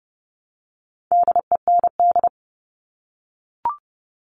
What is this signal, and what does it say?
Morse code, heard as a keyed tone, about 30 wpm, 700 Hz. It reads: DEND
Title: All Cut Numbers - 4-Digit 30wpm